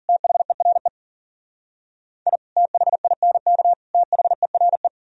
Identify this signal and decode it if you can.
Morse code, heard as a keyed tone, about 51 wpm, 700 Hz. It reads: THERE I THINK THERE